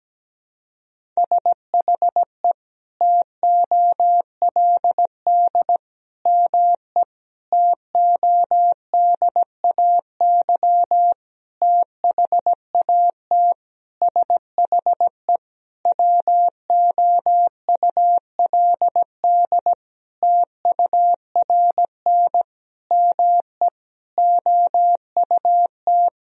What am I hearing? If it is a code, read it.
Morse code, heard as a keyed tone, about 17 wpm, 700 Hz. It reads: SHE TOLD ME TODAY THAT SHE WOULD TURN ME OUT
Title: Sentences from Top 1000 Words - Review Part 4 - 17wpm